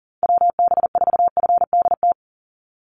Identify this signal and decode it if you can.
Morse code, heard as a keyed tone, about 40 wpm, 700 Hz. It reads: WB4FDT
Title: Call Signs Repeated 2x - 40wpm